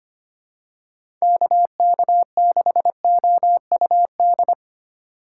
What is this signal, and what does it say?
Morse code, heard as a keyed tone, about 25 wpm, 700 Hz. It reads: KK6OUD
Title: USA Call Signs - 6 Characters - 4x Character Spacing - 25wpm